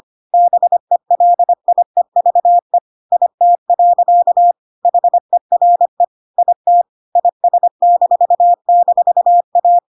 Morse code, heard as a keyed tone, about 25 wpm, 700 Hz. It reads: BELIEVE IT. HERE IT IS--A
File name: Pinocchio / Pinocchio ch1 25 25 ik1hge